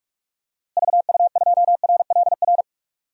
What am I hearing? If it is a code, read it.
Morse code, heard as a keyed tone, about 46 wpm, 700 Hz. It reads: VU2RLR
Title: DXCC-100 Call Signs - 6 Characters - Repeated 3x - 45wpm